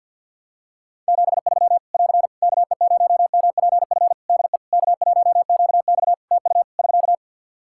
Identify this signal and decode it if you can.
Morse code, heard as a keyed tone, about 48 wpm, 700 Hz. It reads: 73 <AR> KE0MPF DE K1QX TU <SK>
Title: Practice QSO - Exchange 4 - Ending - 50wpm